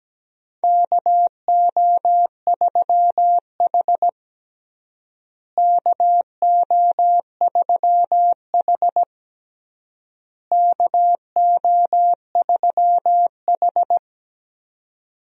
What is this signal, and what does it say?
Morse code, heard as a keyed tone, about 17 wpm, 700 Hz. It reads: KO3H KO3H KO3H
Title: USA Call Signs - 4 Characters - Repeated 3x - 17wpm